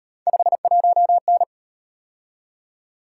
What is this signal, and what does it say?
Morse code, heard as a keyed tone, about 36 wpm, 700 Hz. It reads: 51N